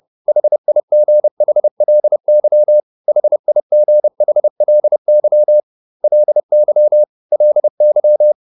Morse code, heard as a keyed tone, about 30 wpm, 600 Hz. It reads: HIGHLY HIGHLY LY LY